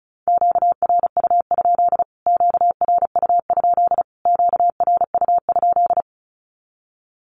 Morse code, heard as a keyed tone, about 34 wpm, 700 Hz. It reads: QRU? QRU? QRU?
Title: Q-Codes Repeated 3x 35wpm